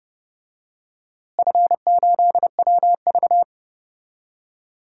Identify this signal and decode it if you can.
Morse code, heard as a keyed tone, about 31 wpm, 700 Hz. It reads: F8WV
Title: DXCC-100 Call Signs - 4 Characters - Repeated 2x - 30wpm